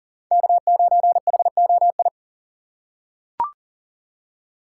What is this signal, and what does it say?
Morse code, heard as a keyed tone, about 40 wpm, 700 Hz. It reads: K9HOI
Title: USA Call Signs - 4x Character Spacing - 40wpm